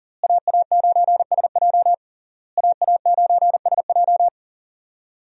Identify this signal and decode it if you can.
Morse code, heard as a keyed tone, about 40 wpm, 700 Hz. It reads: AA9SJ AA9SJ
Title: USA Call Signs - 5 Characters - Repeated 3x - 40wpm